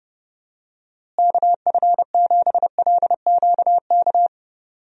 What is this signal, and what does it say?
Morse code, heard as a keyed tone, about 30 wpm, 700 Hz. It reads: KF7LQK
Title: USA Call Signs - 6 Characters - Repeated 3x - 30wpm